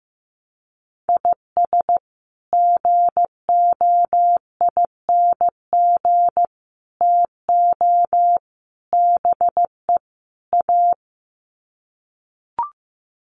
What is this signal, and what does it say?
Morse code, heard as a keyed tone, about 15 wpm, 700 Hz. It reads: IS GOING TO BE A
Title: Sets of 5 Words 15wpm